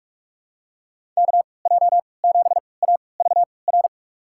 Morse code, heard as a keyed tone, about 46 wpm, spaced as for 26 wpm, 700 Hz. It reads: KJ7AVR
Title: USA Call Signs - 6 Characters - 3x Character Spacing - 45wpm